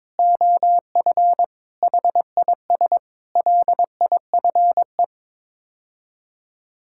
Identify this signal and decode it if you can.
Morse code, heard as a keyed tone, about 22 wpm, 700 Hz. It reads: OF HIS LIFE
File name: practice / 22wpm / Sets of 6 Words - 22wpm